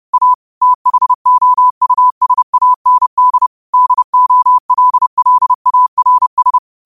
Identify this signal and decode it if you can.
Morse code, heard as a keyed tone, about 30 wpm, 1000 Hz. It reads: A THOUSAND DOLLARS
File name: 30wpm